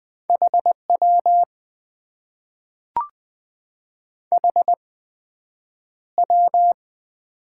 Morse code, heard as a keyed tone, about 21 wpm, spaced as for 9 wpm, 700 Hz. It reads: HW HW
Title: Top 90 QSO Elements - 8x Character Spacing - 20wpm